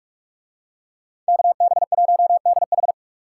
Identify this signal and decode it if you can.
Morse code, heard as a keyed tone, about 46 wpm, 700 Hz. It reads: KB1DH